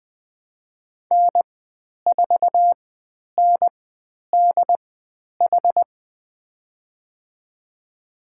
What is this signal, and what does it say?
Morse code, heard as a keyed tone, about 20 wpm, spaced as for 10 wpm, 700 Hz. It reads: N4NDH
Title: 095 Lesson 16g - Letter c - Call Signs 20wpm